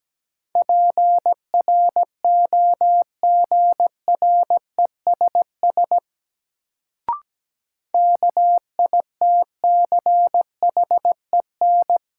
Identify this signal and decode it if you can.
Morse code, heard as a keyed tone, about 17 wpm, 700 Hz. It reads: PROGRESS KITCHEN
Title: Sentences from Top 1400 Words - New Word Drill - 17wpm